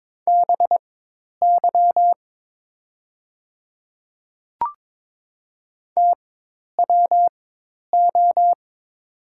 Morse code, heard as a keyed tone, about 22 wpm, spaced as for 10 wpm, 690 Hz. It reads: BY TWO
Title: 4x Character Spacing - Top 100 Words - 22wpm